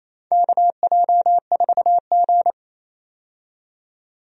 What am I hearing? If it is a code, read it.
Morse code, heard as a keyed tone, about 27 wpm, 700 Hz. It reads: KJ4G
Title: USA Call Signs - 4 Characters - 28wpm